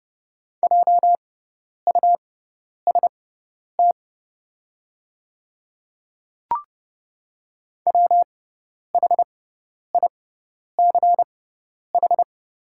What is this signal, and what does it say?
Morse code, heard as a keyed tone, about 30 wpm, spaced as for 10 wpm, 690 Hz. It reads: JUST WHICH
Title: 6x Character Spacing - Top 200 Words - 30wpm